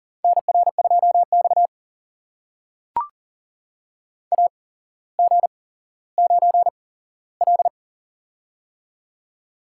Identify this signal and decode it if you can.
Morse code, heard as a keyed tone, about 39 wpm, spaced as for 17 wpm, 690 Hz. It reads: NR2X AG9L
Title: USA Call Signs - 4 Characters - 8x Character Spacing - 40wpm